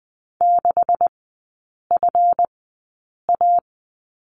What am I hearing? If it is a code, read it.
Morse code, heard as a keyed tone, about 20 wpm, spaced as for 8 wpm, 690 Hz. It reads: 6FA
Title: 191 Lesson 31e - Letter v - Three Characters 20wpm